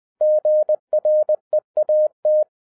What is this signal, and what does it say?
Morse code, heard as a keyed tone, about 20 wpm, 610 Hz. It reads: GREAT